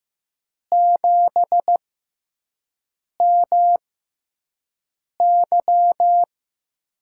Morse code, heard as a keyed tone, about 15 wpm, spaced as for 5 wpm, 700 Hz. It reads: 7MY
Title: USA Call Signs - 6x Character Spacing - 15wpm